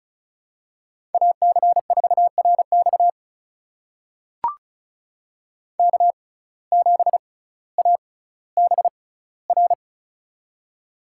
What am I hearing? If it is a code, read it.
Morse code, heard as a keyed tone, about 36 wpm, spaced as for 18 wpm, 700 Hz. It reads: AC4RX K7ABR
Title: USA Call Signs - 5 Characters - 6x Character Spacing - 35wpm